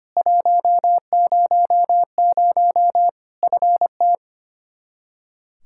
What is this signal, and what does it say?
Morse code, heard as a keyed tone, about 25 wpm, 700 Hz. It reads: 100 FT
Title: Practice QSO - Exchange 2 - Info - 25wpm